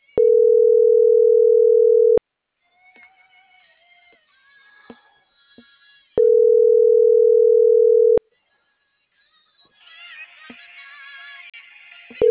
Ringback tone (North America), 440 + 480 Hz.